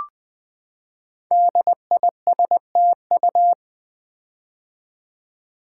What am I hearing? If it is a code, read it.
Morse code, heard as a keyed tone, about 20 wpm, 700 Hz. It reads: DISTU
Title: Build A Word - 5 Letters At A Time - 10-Letter Words - 20wpm